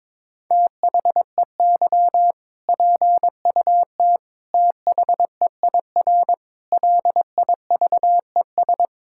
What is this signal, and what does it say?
Morse code, heard as a keyed tone, about 22 wpm, 700 Hz. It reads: THEY PUT THEIR LIVES
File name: practice / 22wpm / Sentences with 8-Word Phrases - 22wpm